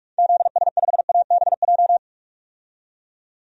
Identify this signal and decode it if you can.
Morse code, heard as a keyed tone, about 46 wpm, 700 Hz. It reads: ZS5ABJ
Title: DXCC-100 Call Signs - 6 Characters - 45wpm